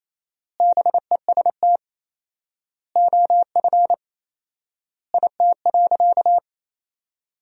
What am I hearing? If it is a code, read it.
Morse code, heard as a keyed tone, about 28 wpm, 700 Hz. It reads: BEST OF IT.
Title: 4x Word-Spacing - Sentences from Top 400 Words - 28wpm